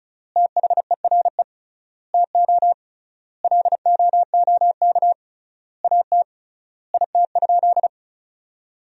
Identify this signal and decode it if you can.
Morse code, heard as a keyed tone, about 36 wpm, 700 Hz. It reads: THERE TO LOOK AT IT?
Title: 3x Word-Spacing - Sentences from Top 100 Words - 35wpm